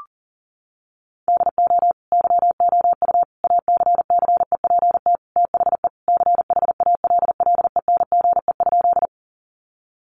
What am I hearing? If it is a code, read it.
Morse code, heard as a keyed tone, about 40 wpm, 700 Hz. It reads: DO YOU ACCEPT THE CHALLENGE?
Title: Sentences from Top 1300 Words 40wpm